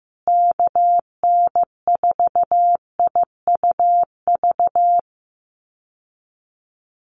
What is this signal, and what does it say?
Morse code, heard as a keyed tone, about 15 wpm, 700 Hz. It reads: KN4IUV
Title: USA Call Signs Repeated 3x 15wpm